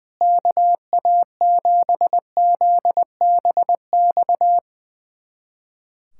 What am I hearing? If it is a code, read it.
Morse code, heard as a keyed tone, about 20 wpm, 700 Hz. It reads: KA7ZBX
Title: USA Callsign DE Callsign - 6 Characters - 20wpm